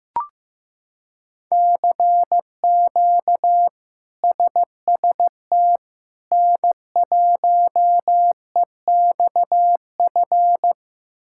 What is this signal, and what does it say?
Morse code, heard as a keyed tone, about 15 wpm, 700 Hz. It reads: CQ SST N1EXF
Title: K1USN SST QSO Practice - 15wpm